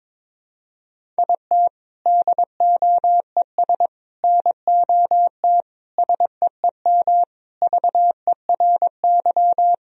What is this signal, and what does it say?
Morse code, heard as a keyed tone, about 22 wpm, 700 Hz. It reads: IT DOES NOT SEEM VERY
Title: Sentences from Top 1000 Words - Review Part 1 - 22wpm